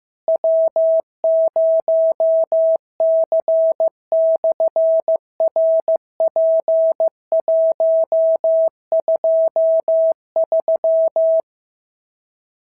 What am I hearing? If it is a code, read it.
Morse code, heard as a keyed tone, about 15 wpm, 650 Hz. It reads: W0C/RP123
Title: SOTA QSO Practice 15wpm